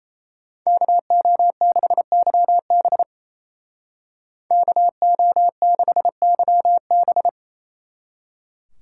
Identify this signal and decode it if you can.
Morse code, heard as a keyed tone, about 33 wpm, 700 Hz. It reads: KO6YB KO6YB
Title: Speed Racing - USA 2x2 Call Signs 28wpm